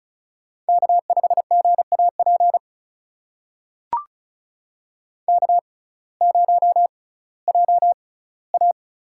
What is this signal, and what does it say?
Morse code, heard as a keyed tone, about 36 wpm, spaced as for 19 wpm, 700 Hz. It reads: K5GAP K0JA
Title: USA Call Signs - 6x Character Spacing - 35wpm